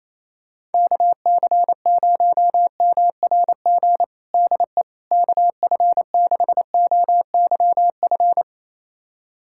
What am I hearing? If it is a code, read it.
Morse code, heard as a keyed tone, about 28 wpm, 700 Hz. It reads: KC0MRG DE KF6OYF
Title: USA Callsign DE Callsign - 6 Characters - 28wpm